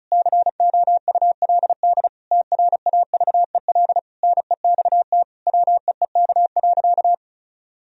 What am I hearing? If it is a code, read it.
Morse code, heard as a keyed tone, about 34 wpm, 700 Hz. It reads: COULD TRAVEL NEXT WEEK.
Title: Sentences from Top 1000 Words - Review Part 2 - 35wpm